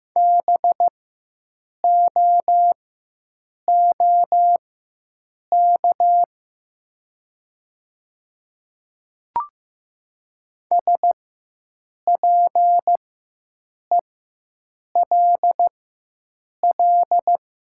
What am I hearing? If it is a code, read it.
Morse code, heard as a keyed tone, about 15 wpm, spaced as for 7 wpm, 700 Hz. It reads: BOOK SPELL
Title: 4x Character Spacing - Top 300 Words - 15wpm